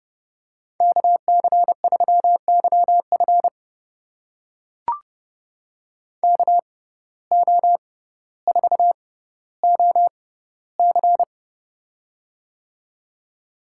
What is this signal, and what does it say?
Morse code, heard as a keyed tone, about 30 wpm, spaced as for 15 wpm, 700 Hz. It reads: KC3YF KO4OC
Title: USA Call Signs - 5 Characters - 6x Character Spacing - 30wpm